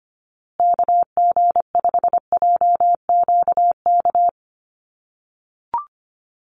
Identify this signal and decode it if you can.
Morse code, heard as a keyed tone, about 25 wpm, 700 Hz. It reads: KG5JQK
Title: USA Call Signs Repeated 2x 25wpm